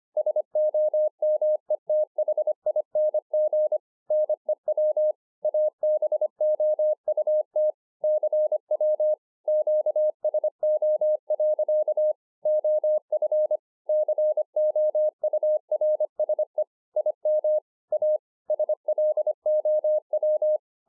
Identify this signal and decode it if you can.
Morse code, heard as a keyed tone, about 26 wpm, 610 Hz. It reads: SOMETHING NEW ABOUT CW QSO. OF COURSE IM A SLOW
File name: 2011-04-mp3 / BEGINNERS DIARY 25WPM